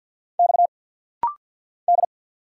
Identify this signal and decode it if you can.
Morse code, heard as a keyed tone, about 50 wpm, 700 Hz. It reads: X D
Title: Single Letters - Rapid-Fire 50wpm